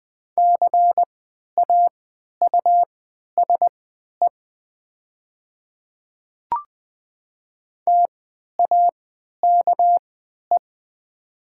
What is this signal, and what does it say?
Morse code, heard as a keyed tone, about 20 wpm, spaced as for 11 wpm, 700 Hz. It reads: CAUSE TAKE
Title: 3x Character Spacing - Top 200 Words - 20wpm